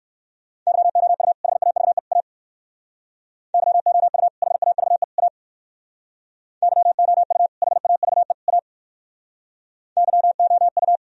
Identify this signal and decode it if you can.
Morse code, heard as a keyed tone, about 58 wpm, 700 Hz. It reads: YOU HAVE A YOU HAVE A YOU HAVE A YOU